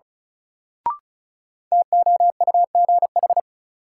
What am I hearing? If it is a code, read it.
Morse code, heard as a keyed tone, about 36 wpm, 700 Hz. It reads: TOUGH
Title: Sentences from Top 1200 Words - New Word Drill - 35wpm